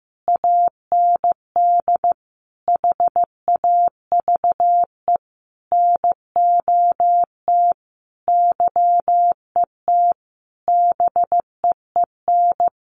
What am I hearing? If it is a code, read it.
Morse code, heard as a keyed tone, about 15 wpm, 700 Hz. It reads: AND HAVE NOT YET BEEN
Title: Sentences from Top 700 Words 15wpm